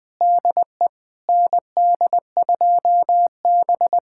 Morse code, heard as a keyed tone, about 21 wpm, 700 Hz. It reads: DE ND2B
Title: USA Callsign DE Callsign - 4 Characters - 20wpm